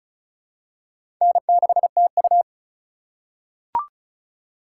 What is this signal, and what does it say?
Morse code, heard as a keyed tone, about 34 wpm, 700 Hz. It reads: N6TU
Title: USA Call Signs - 3x Character Spacing - 35wpm